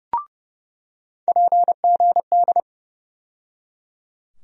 Morse code, heard as a keyed tone, about 30 wpm, 700 Hz. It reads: PGD
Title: USA Call Sign Postfixes - 3 Letters 30wpm